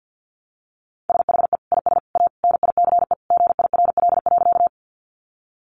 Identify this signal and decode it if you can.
Morse code, heard as a keyed tone, about 49 wpm, 700 Hz. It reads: SHE IS A NICE GIRL.
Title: Sentences from Top 1100 Words 50wpm